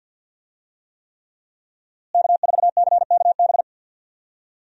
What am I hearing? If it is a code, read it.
Morse code, heard as a keyed tone, about 49 wpm, 700 Hz. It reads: K4CKB
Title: USA Call Signs - 5 Characters - Repeated 3x - 50wpm